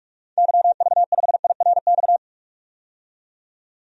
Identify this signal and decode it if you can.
Morse code, heard as a keyed tone, about 46 wpm, 700 Hz. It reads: YV5IRX
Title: DXCC-100 Call Signs - 6 Characters - 45wpm